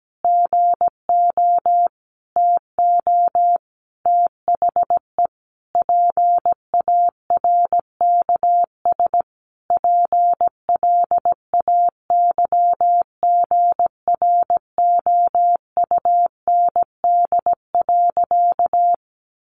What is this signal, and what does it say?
Morse code, heard as a keyed tone, about 17 wpm, 700 Hz. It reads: GO TO THE PARKS PLAYGROUND.
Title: Additional Easy Phrases 17wpm